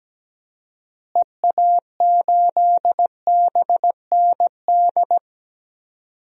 Morse code, heard as a keyed tone, about 17 wpm, 700 Hz. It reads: EA8BND
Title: DXCC-100 Call Signs - 6 Characters - Repeated 3x - 17wpm